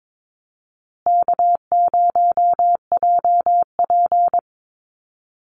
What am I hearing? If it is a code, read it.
Morse code, heard as a keyed tone, about 22 wpm, 700 Hz. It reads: K0JP
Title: Call Signs Repeated 2x - 22wpm